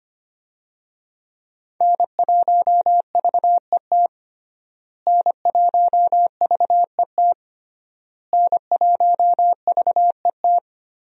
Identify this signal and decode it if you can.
Morse code, heard as a keyed tone, about 25 wpm, 700 Hz. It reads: N1VET N1VET N1VET
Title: USA Call Signs - 5 Characters - Repeated 3x - 25wpm